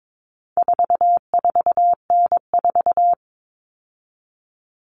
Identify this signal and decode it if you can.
Morse code, heard as a keyed tone, about 22 wpm, 700 Hz. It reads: 44N4
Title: All Cut Numbers - 4-Digit 22wpm